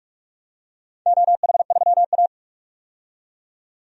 Morse code, heard as a keyed tone, about 45 wpm, 700 Hz. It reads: OH3A
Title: DXCC-100 Call Signs - 4 Characters - 45wpm